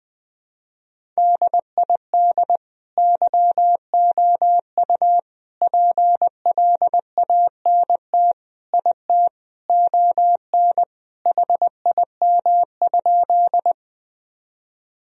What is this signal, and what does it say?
Morse code, heard as a keyed tone, about 20 wpm, 700 Hz. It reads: DID YOU PLANT IT ON HIM?
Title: Sentences from Top 1000 Words - Review Part 1 - 20wpm